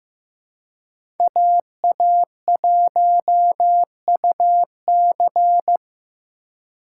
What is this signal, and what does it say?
Morse code, heard as a keyed tone, about 15 wpm, 700 Hz. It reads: AA1UC
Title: K1USN SST QSO Practice - 15wpm